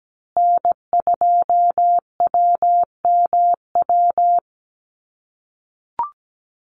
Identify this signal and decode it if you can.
Morse code, heard as a keyed tone, about 17 wpm, 700 Hz. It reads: N2WMW
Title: USA Call Signs Repeated 2x 17wpm